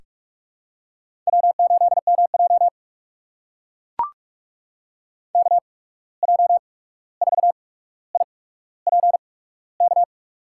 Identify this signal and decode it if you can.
Morse code, heard as a keyed tone, about 44 wpm, spaced as for 17 wpm, 700 Hz. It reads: W8MJ KJ4IPK
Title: USA Call Signs - 8x Character Spacing - 45wpm